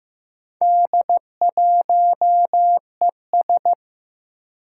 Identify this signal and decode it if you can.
Morse code, heard as a keyed tone, about 15 wpm, 700 Hz. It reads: D1ES
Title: Four Challenging Letter-Number Combinations 15wpm